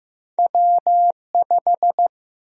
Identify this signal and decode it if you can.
Morse code, heard as a keyed tone, about 15 wpm, 700 Hz. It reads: W5
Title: USA Call Sign Prefixes - Repeated 2x - 15wpm